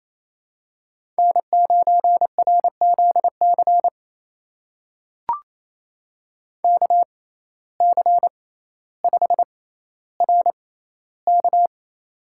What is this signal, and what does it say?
Morse code, heard as a keyed tone, about 28 wpm, spaced as for 14 wpm, 700 Hz. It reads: N9RZC KC5RK
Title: USA Call Signs - 5 Characters - 6x Character Spacing - 28wpm